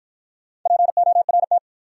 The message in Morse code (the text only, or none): PORT